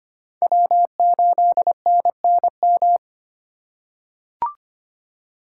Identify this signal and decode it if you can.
Morse code, heard as a keyed tone, about 25 wpm, 700 Hz. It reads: W8NNM
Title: USA Call Signs - 5 Characters - 2x Character Spacing - 25wpm